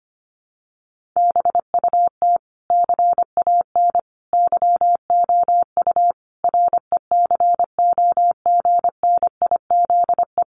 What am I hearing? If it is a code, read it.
Morse code, heard as a keyed tone, about 25 wpm, 700 Hz. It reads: BUT CAN YOU RECOGNIZE
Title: Sentences from Top 1400 Words 25wpm